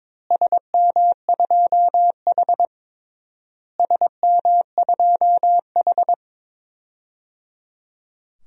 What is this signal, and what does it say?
Morse code, heard as a keyed tone, about 22 wpm, 700 Hz. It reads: SM2H SM2H
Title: DXCC-100 Call Signs - 4 Characters - Repeated 3x - 22wpm